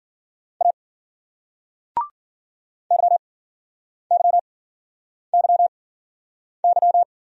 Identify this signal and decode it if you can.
Morse code, heard as a keyed tone, about 53 wpm, spaced as for 9 wpm, 700 Hz. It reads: A YYYY